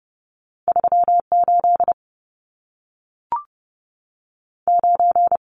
Morse code, heard as a keyed tone, about 30 wpm, 700 Hz. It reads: 3 8 9
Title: Numbers - 2-Digit 30wpm